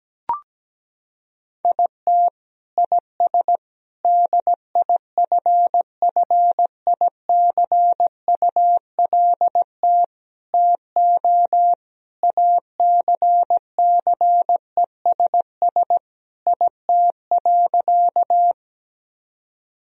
Morse code, heard as a keyed tone, about 17 wpm, 700 Hz. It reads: IT IS DIFFICULT TO ACCESS IT.
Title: Sentences from Top 1000 Words - Review Part 4 - 17wpm